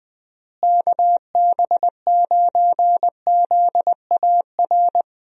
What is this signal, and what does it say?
Morse code, heard as a keyed tone, about 20 wpm, 700 Hz. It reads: KB9ZAR
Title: USA Call Signs - 6 Characters - Repeated 3x - 20wpm